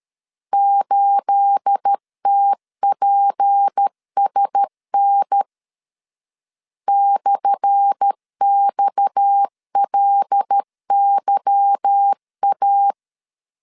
Morse code, heard as a keyed tone, about 13 wpm, 800 Hz. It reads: 8TPSN /XLYA
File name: LEZIONI / 10 SIGNS / 10A SLASH 12wpm 2314